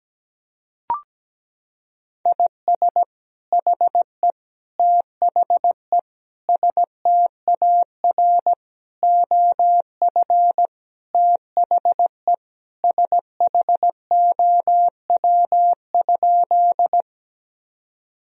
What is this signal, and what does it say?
Morse code, heard as a keyed tone, about 17 wpm, 700 Hz. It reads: IS HE THE STAR OF THE SHOW?
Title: Sentences from Top 1000 Words - Review Part 2 - 17wpm